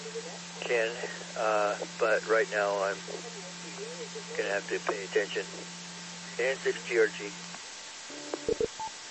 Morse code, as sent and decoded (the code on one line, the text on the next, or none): none